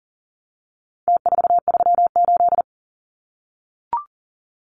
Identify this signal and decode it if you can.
Morse code, heard as a keyed tone, about 40 wpm, 700 Hz. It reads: T438
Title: Numbers - Contest Style - 4-Digit 40wpm